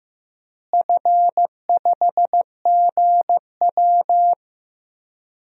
Morse code, heard as a keyed tone, about 15 wpm, 700 Hz. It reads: F5GW
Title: DXCC-100 Call Signs - 4 Characters - Repeated 2x - 15wpm